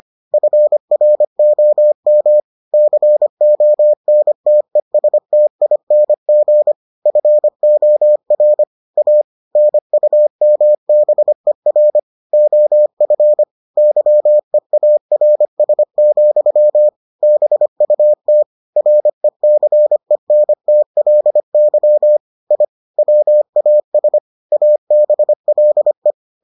Morse code, heard as a keyed tone, about 25 wpm, 600 Hz. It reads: FROM CONTESTING FOR A NUMBER OF YEARS, BUT RECENTLY I WAS ABLE